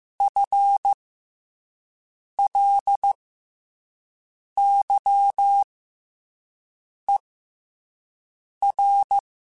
Morse code, heard as a keyed tone, about 15 wpm, spaced as for 5 wpm, 800 Hz. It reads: FLYER
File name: FL Runs / FL Runs mp3s / FL Runs @5